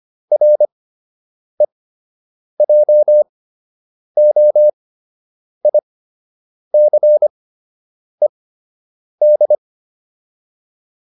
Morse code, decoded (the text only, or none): REJOICED